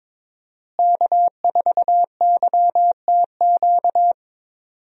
Morse code, heard as a keyed tone, about 22 wpm, 700 Hz. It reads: K4YTQ
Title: USA Call Signs - 5 Characters - Repeated 3x - 22wpm